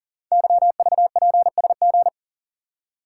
Morse code, heard as a keyed tone, about 43 wpm, 700 Hz. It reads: YVPSG
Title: Code Groups - 5 Letters - 40wpm